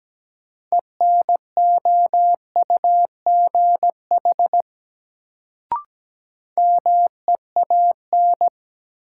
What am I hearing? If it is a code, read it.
Morse code, heard as a keyed tone, about 17 wpm, 700 Hz. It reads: ENOUGH MEAN
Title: Sets of 2 Words - New Word Drill - 17wpm